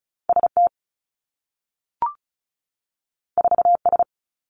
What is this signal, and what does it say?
Morse code, heard as a keyed tone, about 36 wpm, 700 Hz. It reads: ST 4S
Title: Two Challenging Letter-Number Combinations 35wpm